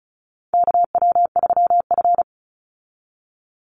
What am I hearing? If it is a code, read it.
Morse code, heard as a keyed tone, about 34 wpm, 700 Hz. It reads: KW3F
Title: Call Signs Repeated 2x - 35wpm